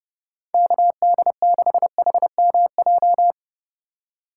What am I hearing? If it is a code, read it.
Morse code, heard as a keyed tone, about 30 wpm, 700 Hz. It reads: KD6HMJ
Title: USA Call Signs - 6 Characters - 4x Character Spacing - 30wpm